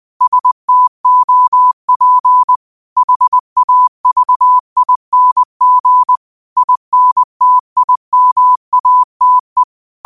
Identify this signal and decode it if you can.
Morse code, heard as a keyed tone, about 20 wpm, 1000 Hz. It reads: STOP HAVING INTIMATE